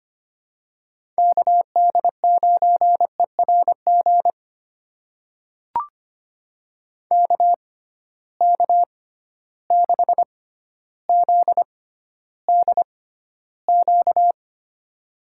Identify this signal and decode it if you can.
Morse code, heard as a keyed tone, about 25 wpm, spaced as for 13 wpm, 700 Hz. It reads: KD9ERG KK6ZDQ
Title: USA Call Signs - 6 Characters - 6x Character Spacing - 25wpm